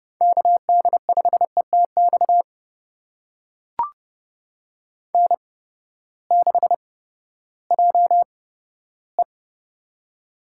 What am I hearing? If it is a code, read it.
Morse code, heard as a keyed tone, about 30 wpm, spaced as for 15 wpm, 700 Hz. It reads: KD5ETX N6JE
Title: USA Call Signs - 8x Character Spacing - 30wpm